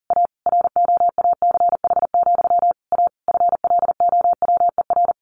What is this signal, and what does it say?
Morse code, heard as a keyed tone, about 40 wpm, 700 Hz. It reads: A ROACH, A FLOWER
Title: A Sound Of Thunder 40wpm